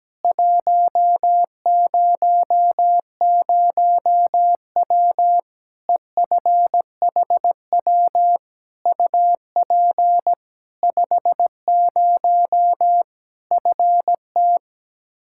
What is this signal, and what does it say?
Morse code, heard as a keyed tone, about 17 wpm, 700 Hz. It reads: 100W EFHW UP 50 FT
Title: Practice QSO - Exchange 2 - Info - 17wpm